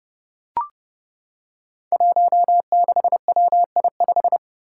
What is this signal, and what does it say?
Morse code, heard as a keyed tone, about 30 wpm, 700 Hz. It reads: 16WI5
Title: Five Challenging Letter-Number Combinations 30wpm